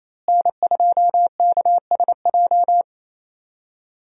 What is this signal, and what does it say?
Morse code, heard as a keyed tone, about 28 wpm, 700 Hz. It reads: N2KSJ
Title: USA Call Signs - 5 Characters - Repeated 3x - 28wpm